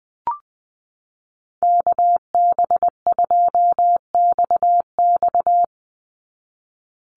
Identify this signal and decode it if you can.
Morse code, heard as a keyed tone, about 19 wpm, 700 Hz. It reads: KB2XX
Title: NA SOTA Chaser Callsigns 20wpm